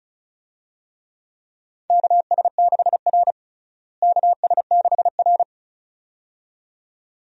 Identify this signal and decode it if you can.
Morse code, heard as a keyed tone, about 36 wpm, 700 Hz. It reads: KS6R KS6R
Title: USA Call Signs - 4 Characters - Repeated 2x - 35wpm